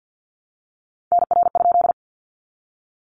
Morse code, heard as a keyed tone, about 48 wpm, 700 Hz. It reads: NR?
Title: Common QSO Phrases 50wpm